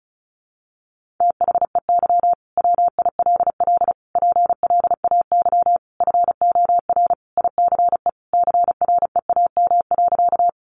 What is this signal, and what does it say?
Morse code, heard as a keyed tone, about 34 wpm, 700 Hz. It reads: THEY WILL PLAY FOR ICE CREAM.
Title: Sentences from Top 1500 Words 35wpm